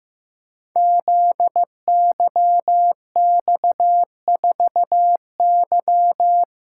Morse code, heard as a keyed tone, about 15 wpm, 700 Hz. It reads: ZYX4Y
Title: Code Groups - 5 Letter:Numbers - 15wpm